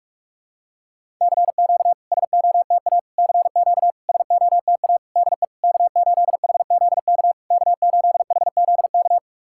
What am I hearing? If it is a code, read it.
Morse code, heard as a keyed tone, about 48 wpm, 700 Hz. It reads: CQ SOTA CQ SOTA DE K8HZK K8HZK
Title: SOTA QSO Practice 45wpm